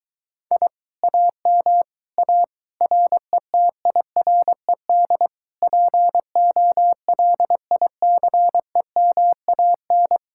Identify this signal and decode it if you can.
Morse code, heard as a keyed tone, about 23 wpm, 700 Hz. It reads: I AM A RETIRED POLICEMAN